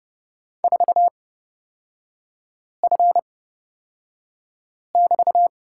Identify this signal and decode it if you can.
Morse code, heard as a keyed tone, about 30 wpm, spaced as for 5 wpm, 700 Hz. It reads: 4F<BT>
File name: course / 30:10wpm / 236 Lesson 38e - Letter q - Three Characters 30wpm